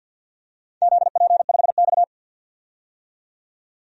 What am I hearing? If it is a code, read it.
Morse code, heard as a keyed tone, about 51 wpm, 700 Hz. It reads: ZP5X
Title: DXCC-100 Call Signs - 4 Characters - 50wpm